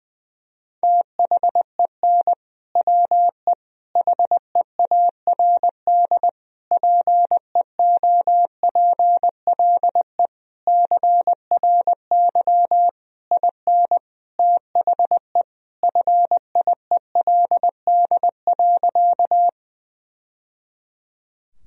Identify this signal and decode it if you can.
Morse code, heard as a keyed tone, about 21 wpm, 700 Hz. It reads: THEN WE HEARD PEOPLE CRY IN THE FIELD.
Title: Sentences from Top 1000 Words - Review Part 2 - 20wpm